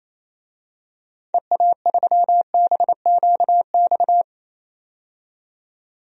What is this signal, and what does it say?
Morse code, heard as a keyed tone, about 28 wpm, 700 Hz. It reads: EA3BQX
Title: DXCC-100 Call Signs - 6 Characters - 28wpm